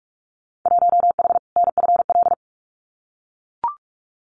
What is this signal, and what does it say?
Morse code, heard as a keyed tone, about 44 wpm, 700 Hz. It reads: 1H NFL
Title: Winter Field Day Exchanges 45wpm